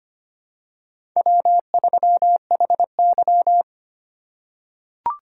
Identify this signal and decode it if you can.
Morse code, heard as a keyed tone, about 25 wpm, 700 Hz. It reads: W3HY
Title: USA Call Signs - 2x Character Spacing - 25wpm